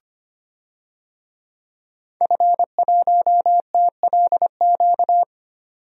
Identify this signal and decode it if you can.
Morse code, heard as a keyed tone, about 25 wpm, 700 Hz. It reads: F1TLQ
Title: DXCC-100 Call Signs - 5 Characters - 25wpm